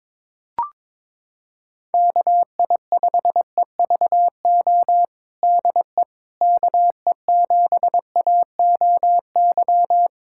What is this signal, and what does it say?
Morse code, heard as a keyed tone, about 22 wpm, 700 Hz. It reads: KI5EVO DE KE7AOY
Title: USA Callsign DE Callsign 22wpm